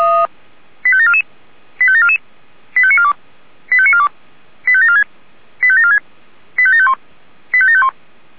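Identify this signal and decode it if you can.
Five-tone selective call (ZVEI), heard as quick runs of five tones, tones of about 70 ms. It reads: 85744 85744 85942 85942 85746 85746 85751 85751
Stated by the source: Received on an SDR as ZVEI five-tone call, address 85827.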